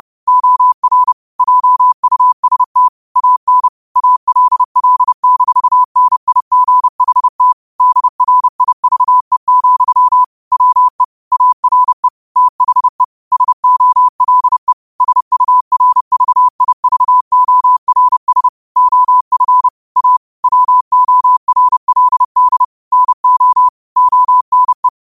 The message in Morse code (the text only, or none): OR JUST AN ALL-NIGHT DRIVE, WE ARE THE SOLE SURVIVORS OF A WORLD NO ONE